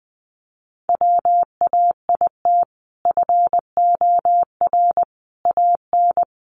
Morse code, heard as a keyed tone, about 19 wpm, 700 Hz. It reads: WAIT FOR AN
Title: Sentences from Top 1000 Words 20wpm